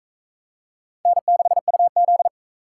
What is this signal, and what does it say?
Morse code, heard as a keyed tone, about 38 wpm, 700 Hz. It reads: N6UZ